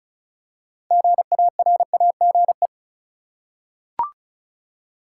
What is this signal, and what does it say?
Morse code, heard as a keyed tone, about 34 wpm, 700 Hz. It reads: GARAGE
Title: Six Letter Words - 35wpm